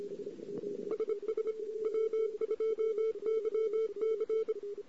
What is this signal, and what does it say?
Morse code, heard as a keyed tone, about 25 wpm, 1320 Hz. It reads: SS W2YC
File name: CW / 2008-ss-cw-k5tr-op-n5rz -14:00